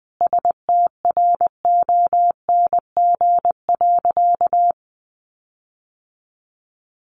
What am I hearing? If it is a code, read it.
Morse code, heard as a keyed tone, about 20 wpm, 700 Hz. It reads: STRONG.
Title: Sentences from Top 1300 Words 20wpm